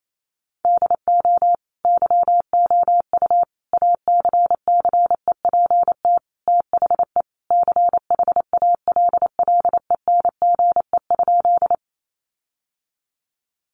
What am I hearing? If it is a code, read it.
Morse code, heard as a keyed tone, about 28 wpm, 700 Hz. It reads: DO YOU ACCEPT THE CHALLENGE?
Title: Sentences from Top 1300 Words 28wpm